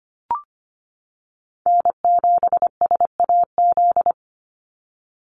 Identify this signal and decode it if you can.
Morse code, heard as a keyed tone, about 25 wpm, 700 Hz. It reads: N7SAZ
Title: USA Call Signs 25wpm